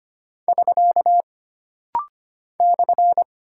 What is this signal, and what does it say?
Morse code, heard as a keyed tone, about 25 wpm, 700 Hz. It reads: <SK> /
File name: Single Punctuation-Prosign - Rapid-Fire - 25wpm